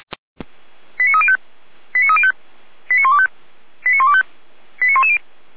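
Five-tone selective call (ZVEI), heard as quick runs of five tones, tones of about 70 ms. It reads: 89305 89305 89125 89125 89110
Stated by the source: ZVEI five-tone selective call received on an SDR.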